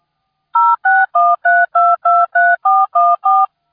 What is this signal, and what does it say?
DTMF, heard as telephone touch tones, tones of about 220 ms. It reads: 0613223414